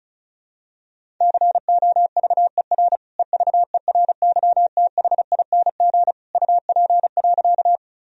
Morse code, heard as a keyed tone, about 36 wpm, 700 Hz. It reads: COVER EVERYTHING UP.
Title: Sentences from Top 1000 Words - Review Part 1 - 35wpm